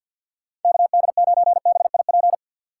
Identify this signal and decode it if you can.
Morse code, heard as a keyed tone, about 52 wpm, 700 Hz. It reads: KD9BIP
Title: USA Call Signs - 6 Characters - 8x Character Spacing - 50wpm